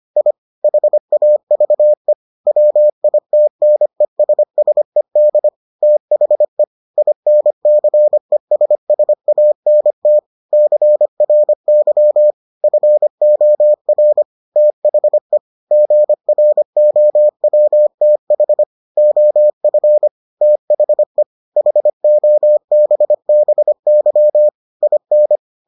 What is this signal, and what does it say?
Morse code, heard as a keyed tone, about 25 wpm, 600 Hz. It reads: I HAVE WITNESSED THE INCESSANT CRY FOR THE GROWTH OF THE HOBBY IN